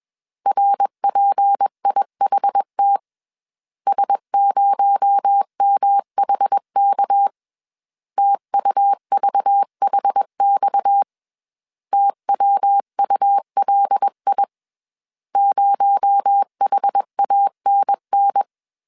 Morse code, heard as a keyed tone, about 21 wpm, 800 Hz. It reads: RPIHT S0MHK TUVHX TWULI 0HANN
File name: CW / DET / 1 6 1500C S A 20wpm 2048